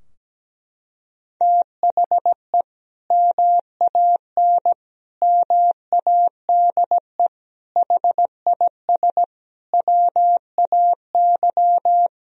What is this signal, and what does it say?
Morse code, heard as a keyed tone, about 17 wpm, 700 Hz. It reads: THE MAN MADE HIS WAY
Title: Sentences from Top 1000 Words - Review Part 2 - 17wpm